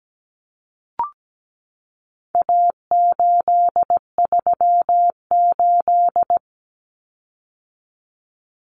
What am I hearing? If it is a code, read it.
Morse code, heard as a keyed tone, about 17 wpm, 700 Hz. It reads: A838
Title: Common Cut Numbers - 4-Digit 17wpm